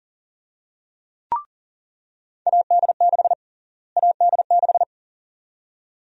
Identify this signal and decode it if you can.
Morse code, heard as a keyed tone, about 37 wpm, 700 Hz. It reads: AD6 AD6
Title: USA Call Sign Prefixes - Repeated 2x - 40wpm